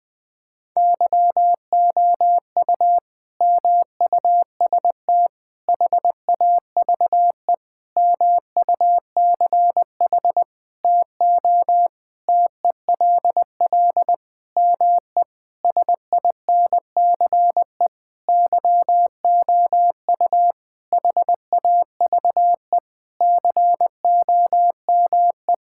Morse code, heard as a keyed tone, about 20 wpm, 700 Hz. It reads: YOU MUST HAVE MUCH TO TELL ME SINCE YOU HAVE COME
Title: Sentences from Top 1600 Words 20wpm